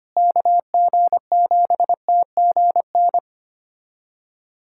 Morse code, heard as a keyed tone, about 25 wpm, 700 Hz. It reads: KG7TGN
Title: USA Callsign DE Callsign 25wpm